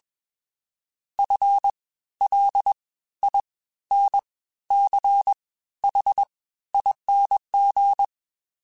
Morse code, heard as a keyed tone, about 21 wpm, spaced as for 13 wpm, 800 Hz. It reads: FLINCHING